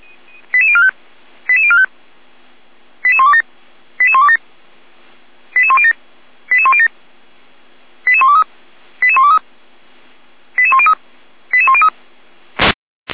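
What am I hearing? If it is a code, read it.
Five-tone selective call (ZVEI), heard as quick runs of five tones, tones of about 70 ms. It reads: 88045 88045 88127 88127 88107 88107 88123 88123 88103 88103
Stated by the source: Five-tone ZVEI selective call received by SDR.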